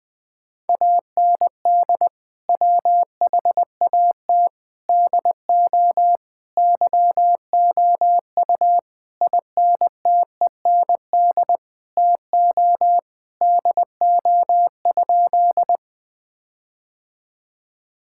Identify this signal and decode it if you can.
Morse code, heard as a keyed tone, about 19 wpm, 700 Hz. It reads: AND WHAT DO YOU INTEND TO DO?
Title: Sentences from Top 1600 Words 20wpm